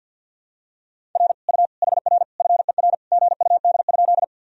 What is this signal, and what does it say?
Morse code, heard as a keyed tone, about 50 wpm, 700 Hz. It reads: R U HR FER GUD?
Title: Sentences from Top 200 Words and Abbreviations 50wpm